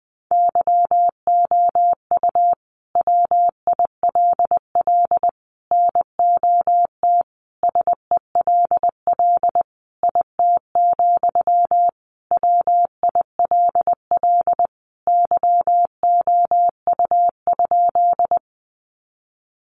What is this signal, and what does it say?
Morse code, heard as a keyed tone, about 20 wpm, 700 Hz. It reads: YOU WILL NOT SELL IT, WILL YOU?
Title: Sentences from Top 1100 Words 20wpm